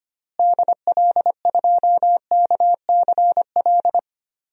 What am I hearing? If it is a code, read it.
Morse code, heard as a keyed tone, about 25 wpm, 700 Hz. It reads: DL2KCL
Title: DXCC-100 Call Signs - 6 Characters - Repeated 3x - 25wpm